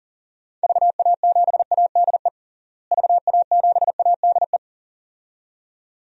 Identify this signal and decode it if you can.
Morse code, heard as a keyed tone, about 40 wpm, 700 Hz. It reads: VA7ADE VA7ADE
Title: DXCC-100 Call Signs - 6 Characters - Repeated 2x - 40wpm